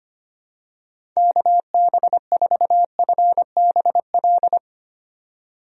Morse code, heard as a keyed tone, about 25 wpm, 700 Hz. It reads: KB4FBL
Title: USA Call Signs - 6 Characters - 25wpm